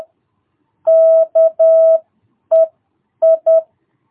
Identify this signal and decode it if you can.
Morse code, heard as a keyed tone, about 10 wpm, spaced as for 8 wpm, 660 Hz. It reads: KEI
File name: jh8cbh / 2022cw